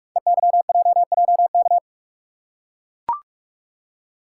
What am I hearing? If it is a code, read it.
Morse code, heard as a keyed tone, about 44 wpm, 700 Hz. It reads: EYJJK
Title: Code Groups - 5 Letters - 45wpm